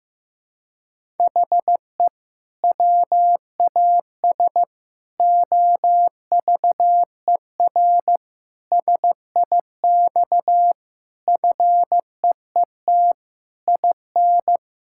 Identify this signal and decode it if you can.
Morse code, heard as a keyed tone, about 15 wpm, 700 Hz. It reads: HE WAS OVER SIX FEET IN
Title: Sentences from Top 1600 Words 15wpm